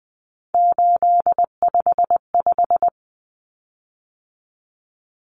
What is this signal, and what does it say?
Morse code, heard as a keyed tone, about 20 wpm, 700 Hz. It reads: 855
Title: Common Cut Numbers - 3-Digit 20wpm